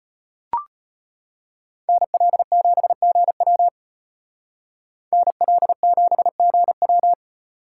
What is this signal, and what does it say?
Morse code, heard as a keyed tone, about 33 wpm, 700 Hz. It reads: NL7GW NL7GW